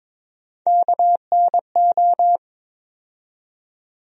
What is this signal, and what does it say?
Morse code, heard as a keyed tone, about 22 wpm, 700 Hz. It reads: KNO
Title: USA Call Sign Postfixes - 3 Letters 22wpm